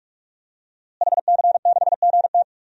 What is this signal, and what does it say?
Morse code, heard as a keyed tone, about 41 wpm, 700 Hz. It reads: HC6GT
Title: DXCC-100 Call Signs - 5 Characters - Repeated 3x - 45wpm